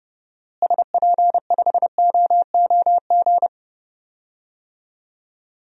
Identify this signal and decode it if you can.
Morse code, heard as a keyed tone, about 32 wpm, 700 Hz. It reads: SP5OOG
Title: DXCC-100 Call Signs - 6 Characters - Repeated 3x - 30wpm